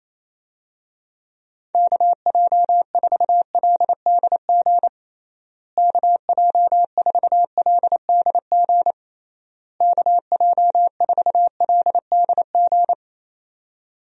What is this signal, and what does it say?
Morse code, heard as a keyed tone, about 28 wpm, 700 Hz. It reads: KJ4LDG KJ4LDG KJ4LDG
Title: USA Call Signs - 6 Characters - Repeated 3x - 28wpm